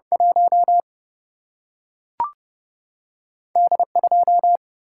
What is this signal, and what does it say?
Morse code, heard as a keyed tone, about 30 wpm, 700 Hz. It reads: 1 D 2
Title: Single-Letter Single-Number 30wpm